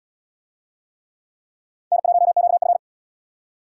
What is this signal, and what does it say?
Morse code, heard as a keyed tone, about 67 wpm, 700 Hz. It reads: N1QF